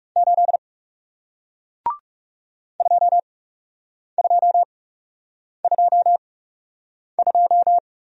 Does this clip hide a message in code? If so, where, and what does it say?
Yes: Morse code from 0.2 to 7.8 s, heard as a keyed tone, about 41 wpm, spaced as for 8 wpm, 700 Hz. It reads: 8 2222